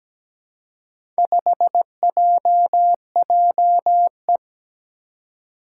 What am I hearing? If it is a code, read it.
Morse code, heard as a keyed tone, about 17 wpm, 700 Hz. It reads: 5JJE
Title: Four Challenging Letter-Number Combinations 17wpm